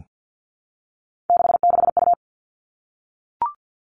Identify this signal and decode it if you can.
Morse code, heard as a keyed tone, about 50 wpm, 700 Hz. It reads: 66U
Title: All Cut Numbers - 3-Digit 50wpm